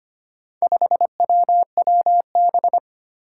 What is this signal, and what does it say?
Morse code, heard as a keyed tone, about 25 wpm, 700 Hz. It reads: 5WWB
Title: Four Challenging Letter-Number Combinations 25wpm